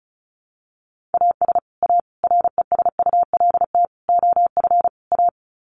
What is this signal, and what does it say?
Morse code, heard as a keyed tone, about 34 wpm, 700 Hz. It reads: AS A RESULT OF A
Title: Sets of 5 Words Encore 35wpm